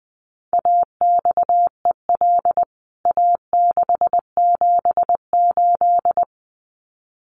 Morse code, heard as a keyed tone, about 20 wpm, 700 Hz. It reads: AXEL A678
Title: CWT Exchange 20wpm